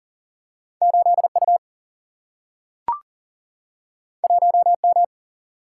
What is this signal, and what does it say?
Morse code, heard as a keyed tone, about 40 wpm, 700 Hz. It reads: 8U 1M
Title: Single-Letter Single-Number 40wpm